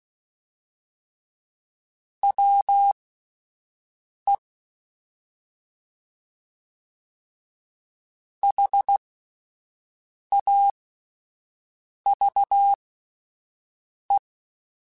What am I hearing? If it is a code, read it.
Morse code, heard as a keyed tone, about 16 wpm, spaced as for 5 wpm, 800 Hz. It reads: WE HAVE
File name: QSOs / QSOs-mp3s / WD0BGP 1@13